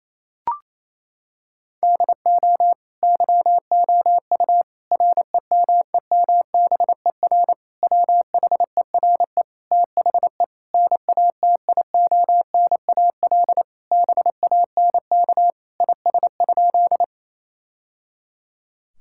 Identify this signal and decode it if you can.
Morse code, heard as a keyed tone, about 28 wpm, 700 Hz. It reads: DO YOU REMEMBER WHERE THE NATIONAL BANK IS?
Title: Sentences from Top 1000 Words - Review Part 6 - 28wpm